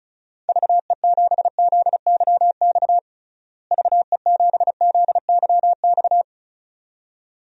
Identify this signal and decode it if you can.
Morse code, heard as a keyed tone, about 36 wpm, 700 Hz. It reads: VE7ZYX VE7ZYX
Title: DXCC-100 Call Signs - 6 Characters - Repeated 2x - 35wpm